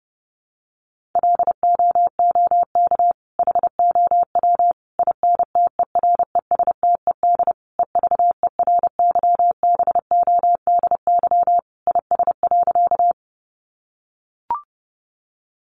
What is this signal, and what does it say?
Morse code, heard as a keyed tone, about 28 wpm, 700 Hz. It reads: LOOK HOW INTERESTED EVERYBODY IS.
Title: Sentences from Top 1400 Words 30wpm